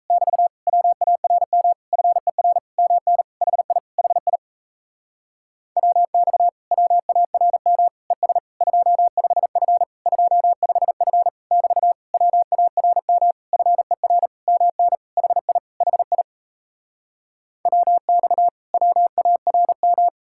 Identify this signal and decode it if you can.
Morse code, heard as a keyed tone, about 37 wpm, 700 Hz. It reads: <BT> WARM FER MN HI HI WX WARM ES 25F 25F <BT> WARM FER MN HI HI WX WARM